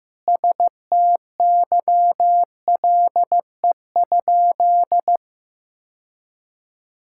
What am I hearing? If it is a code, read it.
Morse code, heard as a keyed tone, about 16 wpm, 700 Hz. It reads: STYLE?
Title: Sentences from Top 1000 Words - Review Part 5 - 15wpm